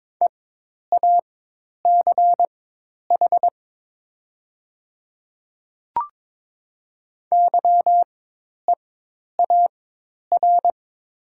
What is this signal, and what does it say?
Morse code, heard as a keyed tone, about 22 wpm, spaced as for 10 wpm, 700 Hz. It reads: EACH YEAR
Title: 4x Character Spacing - Top 200 Words - 22wpm